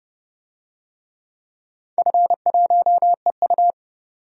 Morse code, heard as a keyed tone, about 30 wpm, 700 Hz. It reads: F1EU
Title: DXCC-100 Call Signs - 4 Characters - 30wpm